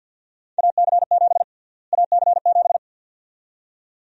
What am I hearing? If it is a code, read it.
Morse code, heard as a keyed tone, about 53 wpm, 700 Hz. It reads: AC7 AC7
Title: USA Call Sign Prefixes - Repeated 2x - 50wpm